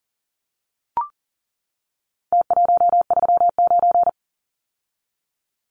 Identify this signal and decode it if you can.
Morse code, heard as a keyed tone, about 40 wpm, 700 Hz. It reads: T139
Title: Numbers - Contest Style - 4-Digit 40wpm